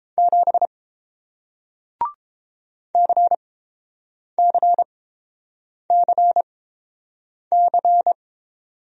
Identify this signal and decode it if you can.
Morse code, heard as a keyed tone, about 30 wpm, spaced as for 7 wpm, 700 Hz. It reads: 7 CCCC